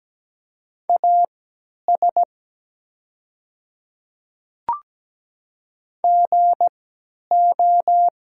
Morse code, heard as a keyed tone, about 17 wpm, spaced as for 10 wpm, 700 Hz. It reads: AS GO
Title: 3x Character Spacing - Top 100 Words - 17wpm